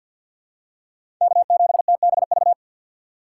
Morse code, heard as a keyed tone, about 49 wpm, 700 Hz. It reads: K7TBV
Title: USA Call Signs - 5 Characters - 4x Character Spacing - 50wpm